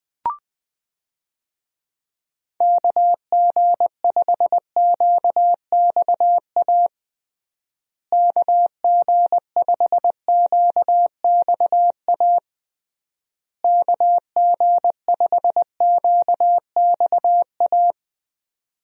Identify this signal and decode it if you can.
Morse code, heard as a keyed tone, about 21 wpm, 700 Hz. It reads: KG5QXA KG5QXA KG5QXA
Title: USA Call Signs - 6 Characters - Repeated 3x - 20wpm